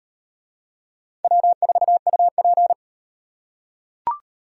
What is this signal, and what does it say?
Morse code, heard as a keyed tone, about 38 wpm, 700 Hz. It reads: W4UP